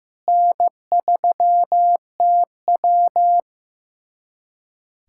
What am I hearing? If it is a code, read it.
Morse code, heard as a keyed tone, about 15 wpm, 700 Hz. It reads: N3TW
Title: USA Callsign DE Callsign - 4 Characters - 15wpm